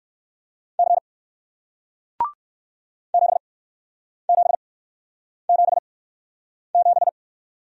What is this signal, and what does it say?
Morse code, heard as a keyed tone, about 54 wpm, spaced as for 9 wpm, 700 Hz. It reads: C 7777
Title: Speed-Racing Single Letter-Number 45wpm